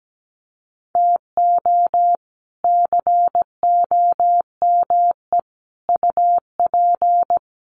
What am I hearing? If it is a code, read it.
Morse code, heard as a keyed tone, about 17 wpm, 700 Hz. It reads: TO COME UP
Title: Sets of 5 Words 17wpm